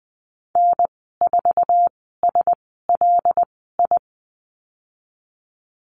Morse code, heard as a keyed tone, about 20 wpm, spaced as for 15 wpm, 700 Hz. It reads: N4SLI
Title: 154 Lesson 25g - Letter p - Call Signs 20wpm